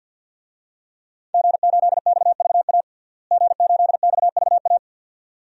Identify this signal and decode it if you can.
Morse code, heard as a keyed tone, about 52 wpm, 700 Hz. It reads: G8XVA G8XVA
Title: DXCC-100 Call Signs - 5 Characters - Repeated 2x - 50wpm